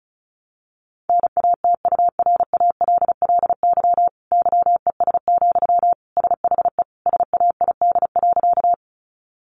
Morse code, heard as a keyed tone, about 34 wpm, 700 Hz. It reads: NATURALLY YES, SHE SAID.
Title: Sentences from Top 1300 Words 35wpm